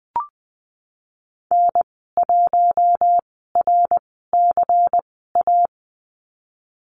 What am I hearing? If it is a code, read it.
Morse code, heard as a keyed tone, about 20 wpm, spaced as for 15 wpm, 700 Hz. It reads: N1RCA
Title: 122 Lesson 20g - Number 3 - Call Signs 20wpm